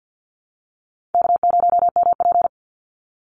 Morse code, heard as a keyed tone, about 50 wpm, 700 Hz. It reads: K0MP
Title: Easy Call Signs - 50wpm